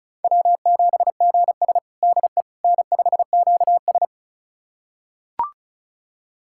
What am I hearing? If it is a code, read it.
Morse code, heard as a keyed tone, about 36 wpm, 700 Hz. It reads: W7GS DE N5QS
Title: USA Callsign DE Callsign - 4 Characters - 35wpm